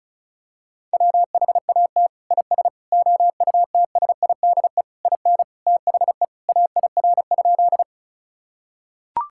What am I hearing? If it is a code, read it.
Morse code, heard as a keyed tone, about 36 wpm, 700 Hz. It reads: WHAT IS OUTSIDE IN THE AIR?
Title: Sentences from Top 1000 Words - Review Part 6 - 35wpm